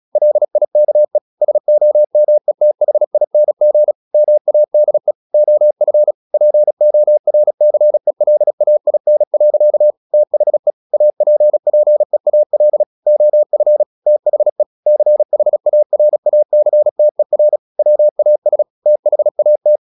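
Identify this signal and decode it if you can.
Morse code, heard as a keyed tone, about 36 wpm, 600 Hz. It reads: LIKE SOMETHING MADE OF PORCELAIN. THE APPEAL OF THE CHARACTER WAS THAT
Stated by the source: Morse practice at 36 wpm, text of Ken Follett's novel Never, from CW med Gnister.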